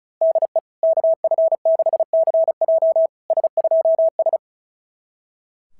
Morse code, heard as a keyed tone, about 36 wpm, 650 Hz. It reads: DE KF6CJ S2S
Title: SOTA QSO Practice 35wpm